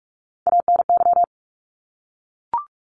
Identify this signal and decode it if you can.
Morse code, heard as a keyed tone, about 44 wpm, 700 Hz. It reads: ANY
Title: Top 300 Words Repeated 45wpm